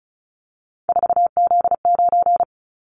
Morse code, heard as a keyed tone, about 34 wpm, 700 Hz. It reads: 4Z9
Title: Callsign Prefixes for Top 50 DXCC - 35wpm